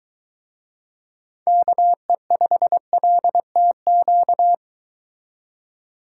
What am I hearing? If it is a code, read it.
Morse code, heard as a keyed tone, about 23 wpm, 700 Hz. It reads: KE5LTQ